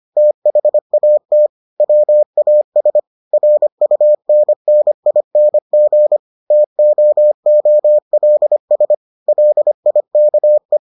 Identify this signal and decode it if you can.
Morse code, heard as a keyed tone, about 25 wpm, 600 Hz. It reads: THAT WAS RUNNING TOOLS LIKE